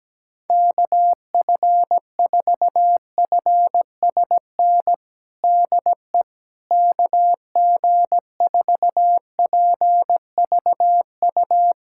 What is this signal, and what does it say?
Morse code, heard as a keyed tone, about 17 wpm, 700 Hz. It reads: KF4FSN DE KG4PVU
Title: USA Callsign DE Callsign - 6 Characters - 17wpm